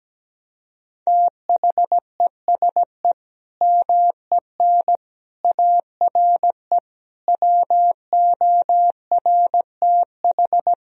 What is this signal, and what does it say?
Morse code, heard as a keyed tone, about 17 wpm, 700 Hz. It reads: THESE MEN ARE WORTH
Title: Sentences from Top 1000 Words - Review Part 1 - 17wpm